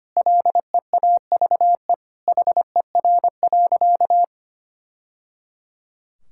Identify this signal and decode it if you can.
Morse code, heard as a keyed tone, about 25 wpm, 700 Hz. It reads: LEAVE HER.
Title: Sentences from Top 1000 Words - Review Part 5 - 25wpm